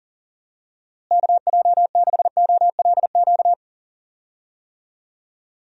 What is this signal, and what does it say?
Morse code, heard as a keyed tone, about 38 wpm, 700 Hz. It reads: KJ6OLQ
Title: USA Call Signs - 6 Characters - Repeated 2x - 40wpm